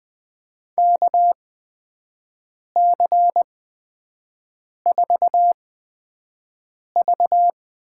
Morse code, heard as a keyed tone, about 20 wpm, spaced as for 5 wpm, 700 Hz. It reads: KC4V
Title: USA Call Signs - 5 Characters - 8x Character Spacing - 20wpm